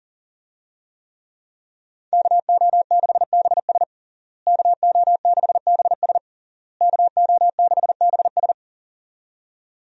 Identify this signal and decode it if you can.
Morse code, heard as a keyed tone, about 40 wpm, 700 Hz. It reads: KO6BS KO6BS KO6BS
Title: USA Call Signs - 5 Characters - Repeated 3x - 40wpm